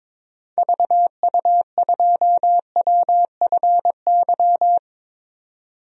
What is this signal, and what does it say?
Morse code, heard as a keyed tone, about 22 wpm, 700 Hz. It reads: VU2WFY
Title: DXCC-100 Call Signs - 6 Characters - 22wpm